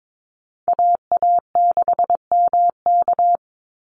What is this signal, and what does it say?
Morse code, heard as a keyed tone, about 22 wpm, 700 Hz. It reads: AA6MK
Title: Call Signs Repeated 2x - 22wpm